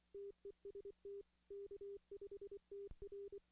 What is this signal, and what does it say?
Morse code, heard as a keyed tone, about 24 wpm, 400 Hz. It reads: TEST K5TR
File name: 2016 arrl dx cw / 2016-ARRL-DX-CW-K5TR- - 0-14